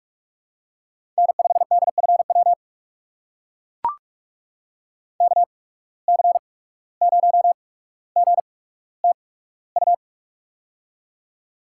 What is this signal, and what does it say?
Morse code, heard as a keyed tone, about 44 wpm, spaced as for 18 wpm, 690 Hz. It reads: N5DFW KC0GTU
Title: USA Call Signs - 8x Character Spacing - 45wpm